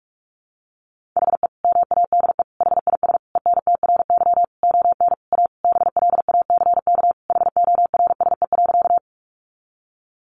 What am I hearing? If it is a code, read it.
Morse code, heard as a keyed tone, about 44 wpm, 700 Hz. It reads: HE MADE HIS ENTRY ON A BLACK HORSE.
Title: Sentences from Top 1500 Words 45wpm